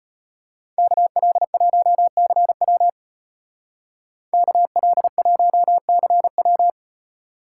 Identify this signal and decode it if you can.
Morse code, heard as a keyed tone, about 31 wpm, 700 Hz. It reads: KL1CW KL1CW